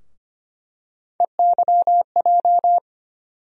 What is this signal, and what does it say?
Morse code, heard as a keyed tone, about 25 wpm, 700 Hz. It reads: EYJ
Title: USA Call Sign Postfixes - 3 Letters 25wpm